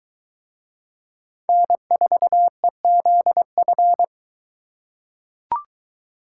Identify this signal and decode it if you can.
Morse code, heard as a keyed tone, about 22 wpm, 700 Hz. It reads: N4EZF